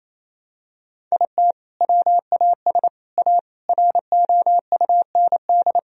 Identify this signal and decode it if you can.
Morse code, heard as a keyed tone, about 27 wpm, 700 Hz. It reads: IT WAS A ROUND
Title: Sentences from Top 1000 Words - Review Part 2 - 28wpm